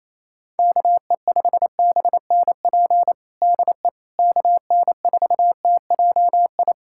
Morse code, heard as a keyed tone, about 28 wpm, 700 Hz. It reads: KE5BNP DE KN4TJI
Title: USA Callsign DE Callsign 28wpm